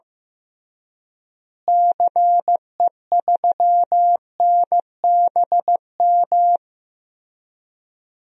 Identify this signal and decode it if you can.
Morse code, heard as a keyed tone, about 15 wpm, 700 Hz. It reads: CE3NBM
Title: DXCC-100 Call Signs - 6 Characters - Repeated 3x - 15wpm